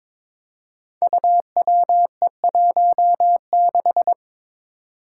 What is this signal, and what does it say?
Morse code, heard as a keyed tone, about 23 wpm, 700 Hz. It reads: UWE16
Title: Five Challenging Letter-Number Combinations 22wpm